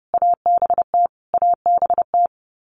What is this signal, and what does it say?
Morse code, heard as a keyed tone, about 30 wpm, 700 Hz. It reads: ABT ABT
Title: Abbreviations Repeated 3x - 30wpm